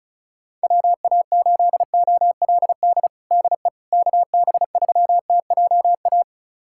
Morse code, heard as a keyed tone, about 34 wpm, 700 Hz. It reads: WA8OLD DE KB3TJA
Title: USA Callsign DE Callsign - 6 Characters - 35wpm